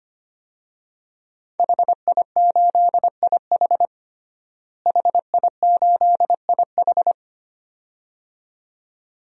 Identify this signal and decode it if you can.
Morse code, heard as a keyed tone, about 25 wpm, 700 Hz. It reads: HI8IH HI8IH
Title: DXCC-100 Call Signs - 5 Characters - Repeated 2x - 25wpm